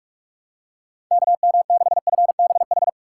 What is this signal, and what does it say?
Morse code, heard as a keyed tone, about 41 wpm, 700 Hz. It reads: KM6FBH
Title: USA Call Signs - 6 Characters - 2x Character Spacing - 45wpm